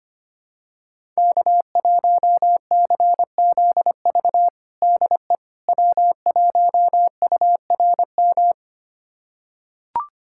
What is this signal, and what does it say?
Morse code, heard as a keyed tone, about 25 wpm, 700 Hz. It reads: K1CZV DE W1URM
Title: USA Callsign DE Callsign - 5 Characters - 25wpm